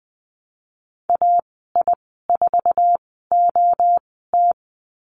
Morse code, heard as a keyed tone, about 20 wpm, spaced as for 14 wpm, 700 Hz. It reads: AI4OT
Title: 069 Lesson 12g - Letter d - Call Signs 20wpm